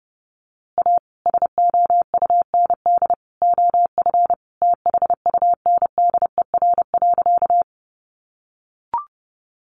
Morse code, heard as a keyed tone, about 30 wpm, 700 Hz. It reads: A SOUND OF THUNDER.
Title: A Sound Of Thunder 30wpm